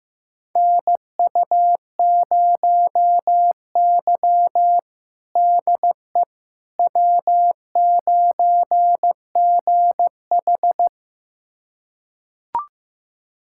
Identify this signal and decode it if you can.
Morse code, heard as a keyed tone, about 15 wpm, 700 Hz. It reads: NU0Y DE W9GH
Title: USA Callsign DE Callsign - 4 Characters - 15wpm